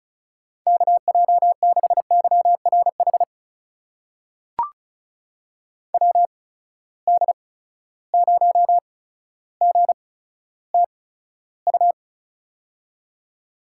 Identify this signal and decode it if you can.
Morse code, heard as a keyed tone, about 34 wpm, spaced as for 15 wpm, 700 Hz. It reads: KJ6YRH WD0GTU
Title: USA Call Signs - 6 Characters - 8x Character Spacing - 35wpm